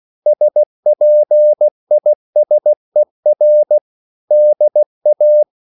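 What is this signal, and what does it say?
Morse code, heard as a keyed tone, about 16 wpm, 600 Hz. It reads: SPISER DA
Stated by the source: Slow (QRS) Morse code at 16 wpm, from CW med Gnister.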